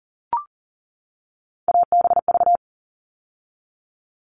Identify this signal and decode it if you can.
Morse code, heard as a keyed tone, about 40 wpm, 700 Hz. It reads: ABV
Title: All Cut Numbers - 3-Digit 40wpm